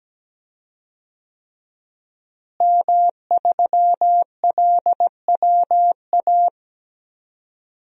Morse code, heard as a keyed tone, about 17 wpm, 700 Hz. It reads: M3LWA
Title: DXCC-100 Call Signs - 5 Characters - Repeated 3x - 17wpm